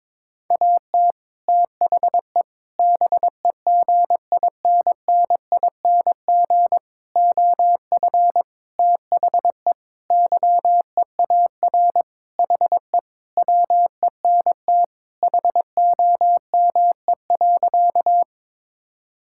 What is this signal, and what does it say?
Morse code, heard as a keyed tone, about 22 wpm, 700 Hz. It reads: AT THE BEGINNING OF THE YEAR HE WENT HOME.
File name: practice / 22wpm / Sentences with 4-Word Phrases - 22wpm